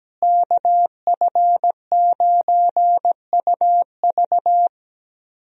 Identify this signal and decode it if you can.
Morse code, heard as a keyed tone, about 17 wpm, 700 Hz. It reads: KF9UV
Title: USA Call Signs - 5 Characters - Repeated 3x - 17wpm